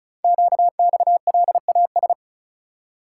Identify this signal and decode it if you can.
Morse code, heard as a keyed tone, about 36 wpm, 700 Hz. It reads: QXLAS
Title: Code Groups - 5 Letters - 35wpm